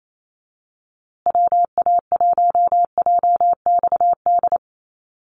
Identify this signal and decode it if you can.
Morse code, heard as a keyed tone, about 28 wpm, 700 Hz. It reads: WA1JXD
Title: USA Call Signs Repeated 2x 28wpm